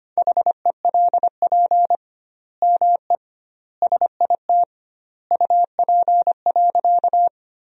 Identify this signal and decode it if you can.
Morse code, heard as a keyed tone, about 25 wpm, 700 Hz. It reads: HELP ME SIT UP.
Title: 2x Word-Spacing - Sentences from Top 500 Words - 25wpm